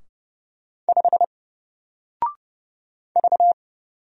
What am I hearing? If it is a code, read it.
Morse code, heard as a keyed tone, about 30 wpm, 700 Hz. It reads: 5 V
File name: Single-Letter-Number-30wpm